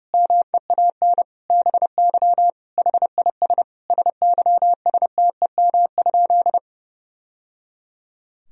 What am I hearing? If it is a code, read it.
Morse code, heard as a keyed tone, about 32 wpm, 700 Hz. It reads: MEAN BY HIS SYSTEM?
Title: Sentences from Top 1000 Words - Review Part 3 - 30wpm